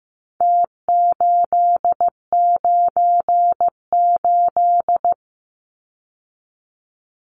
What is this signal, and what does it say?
Morse code, heard as a keyed tone, about 15 wpm, 700 Hz. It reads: T898
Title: Numbers - Contest Style - 4-Digit 15wpm